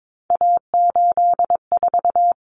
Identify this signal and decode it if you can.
Morse code, heard as a keyed tone, about 22 wpm, 700 Hz. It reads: A84
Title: Common Cut Numbers - 3-Digit 22wpm